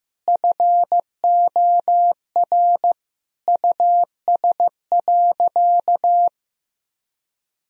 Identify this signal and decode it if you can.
Morse code, heard as a keyed tone, about 15 wpm, 700 Hz. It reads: FOR US.
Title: Sentences from Top 1000 Words - Review Part 2 - 15wpm